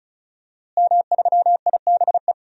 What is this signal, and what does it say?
Morse code, heard as a keyed tone, about 36 wpm, 700 Hz. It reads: M3IBE
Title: DXCC-100 Call Signs - 5 Characters - Repeated 3x - 35wpm